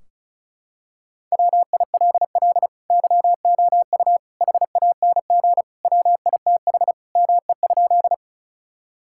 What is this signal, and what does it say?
Morse code, heard as a keyed tone, about 34 wpm, 700 Hz. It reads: WILL YOU HANG WITH ME?
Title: Sentences from Top 1600 Words 35wpm